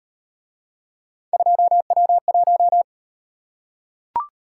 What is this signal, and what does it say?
Morse code, heard as a keyed tone, about 36 wpm, 700 Hz. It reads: 2W1